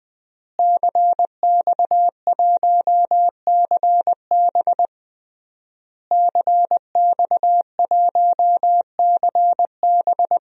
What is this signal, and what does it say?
Morse code, heard as a keyed tone, about 20 wpm, 700 Hz. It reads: CX1CB CX1CB
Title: DXCC-100 Call Signs - 5 Characters - Repeated 3x - 20wpm